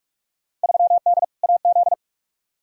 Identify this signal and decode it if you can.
Morse code, heard as a keyed tone, about 42 wpm, 700 Hz. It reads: 3D AZ
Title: Field Day Exchanges - 45wpm